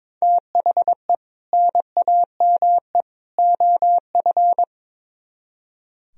Morse code, heard as a keyed tone, about 24 wpm, 700 Hz. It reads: THE NAME OF
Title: Speed-Racing - Sets of 3 Words 22wpm